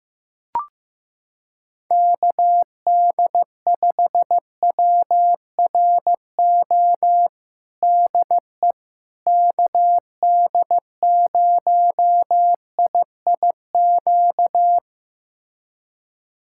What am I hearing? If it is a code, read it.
Morse code, heard as a keyed tone, about 15 wpm, 700 Hz. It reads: KD5WRO DE KD0IIQ
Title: USA Callsign DE Callsign - 6 Characters - 15wpm